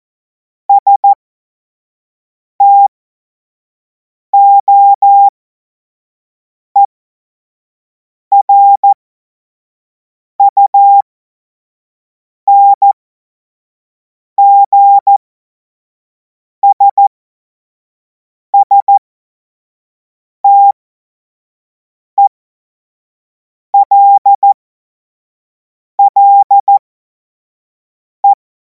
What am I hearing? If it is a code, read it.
Morse code, heard as a keyed tone, about 14 wpm, spaced as for 5 wpm, 800 Hz. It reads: STOERUNGSSTELLE
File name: Morsekurs / Witze / Witz 10. März 2026 25 F70